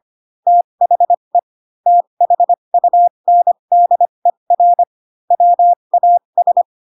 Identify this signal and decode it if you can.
Morse code, heard as a keyed tone, about 25 wpm, spaced as for 22 wpm, 700 Hz. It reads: THE THUNDER WAS
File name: Pinocchio / Pinocchio ch7 18 25 ik1hge